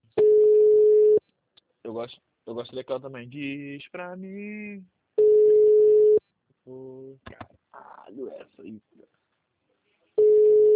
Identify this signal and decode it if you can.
Ringback tone (Europe, 425 Hz), with speech.